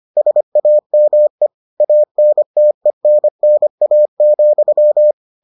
Morse code, heard as a keyed tone, about 25 wpm, 600 Hz. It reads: SAME ANTENNA,